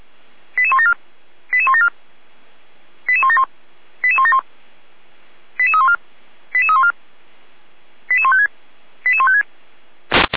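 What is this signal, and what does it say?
Five-tone selective call (ZVEI), heard as quick runs of five tones, tones of about 70 ms. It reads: 88174 88174 88171 88171 88314 88314 88156 88156
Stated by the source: ZVEI five-tone call received by SDR.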